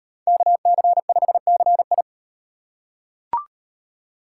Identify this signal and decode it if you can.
Morse code, heard as a keyed tone, about 36 wpm, 700 Hz. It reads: KC5CI